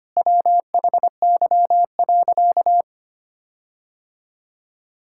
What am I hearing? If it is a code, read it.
Morse code, heard as a keyed tone, about 25 wpm, 700 Hz. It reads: WHY.
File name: practice / 25wpm / Sentences with 4-Word Phrases - 25wpm